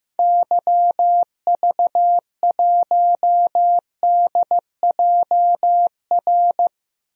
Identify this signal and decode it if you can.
Morse code, heard as a keyed tone, about 15 wpm, 700 Hz. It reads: YV1DJR
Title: DXCC-100 Call Signs - 6 Characters - 15wpm